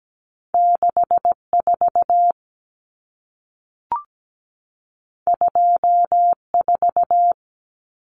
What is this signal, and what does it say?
Morse code, heard as a keyed tone, about 17 wpm, 700 Hz. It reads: 64 24
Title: Numbers - 2-Digit 17wpm